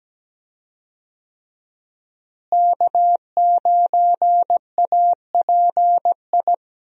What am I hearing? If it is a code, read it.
Morse code, heard as a keyed tone, about 17 wpm, 700 Hz. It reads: K9API
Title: USA Call Signs - 5 Characters - Repeated 3x - 17wpm